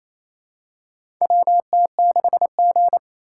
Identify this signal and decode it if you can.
Morse code, heard as a keyed tone, about 27 wpm, 690 Hz. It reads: WT6G
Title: USA Call Signs - 4 Characters - 2x Character Spacing - 28wpm